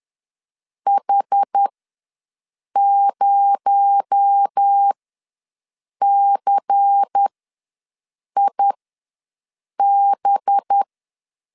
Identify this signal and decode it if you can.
Morse code, heard as a keyed tone, about 10 wpm, spaced as for 6 wpm, 800 Hz. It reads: H0CIB
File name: LEZIONI / 6 PKX0 / 6 PKX0 ws 10wpm 4220